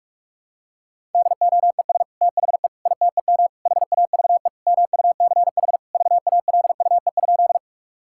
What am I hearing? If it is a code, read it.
Morse code, heard as a keyed tone, about 42 wpm, 700 Hz. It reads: DOES THE ITEM HAVE MUCH VALUE?
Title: Sentences from Top 1000 Words - Review Part 6 - 45wpm